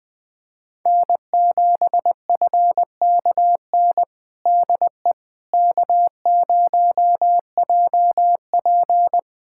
Morse code, heard as a keyed tone, about 20 wpm, 700 Hz. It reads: N7FKN DE K0JP
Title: Practice QSO - Exchange 4 - Ending - 20wpm